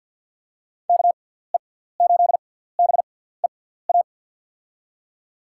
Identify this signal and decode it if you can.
Morse code, heard as a keyed tone, about 48 wpm, spaced as for 17 wpm, 700 Hz. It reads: KE8BEA
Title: USA Call Signs - 6 Characters - 6x Character Spacing - 50wpm